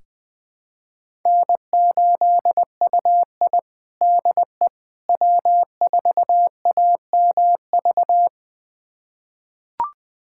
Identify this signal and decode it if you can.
Morse code, heard as a keyed tone, about 21 wpm, 700 Hz. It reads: N8UI DE W4AMV
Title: USA Callsign DE Callsign 20wpm